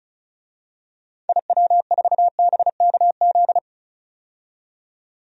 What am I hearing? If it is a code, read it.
Morse code, heard as a keyed tone, about 34 wpm, 700 Hz. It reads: IW4BKZ
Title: DXCC-100 Call Signs - 6 Characters - 35wpm